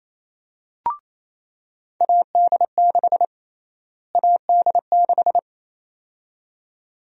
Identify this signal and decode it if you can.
Morse code, heard as a keyed tone, about 28 wpm, 700 Hz. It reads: AD6 AD6
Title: USA Call Sign Prefixes - Repeated 2x - 28wpm